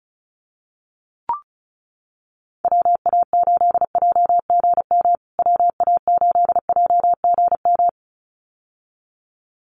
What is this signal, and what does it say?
Morse code, heard as a keyed tone, about 34 wpm, 700 Hz. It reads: WA8JGM WA8JGM
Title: USA Call Signs Repeated 2x 35wpm